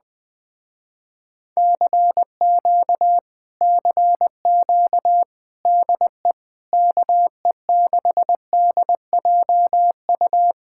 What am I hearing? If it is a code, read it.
Morse code, heard as a keyed tone, about 20 wpm, 700 Hz. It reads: CQ CQ DE KE6DJU
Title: Speed-Racing Common QSO Phrases - 20wpm